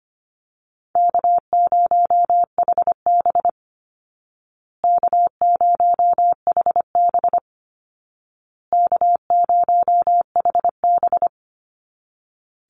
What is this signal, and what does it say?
Morse code, heard as a keyed tone, about 25 wpm, 700 Hz. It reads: K0HB K0HB K0HB
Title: Call Signs Repeated 2x - 25wpm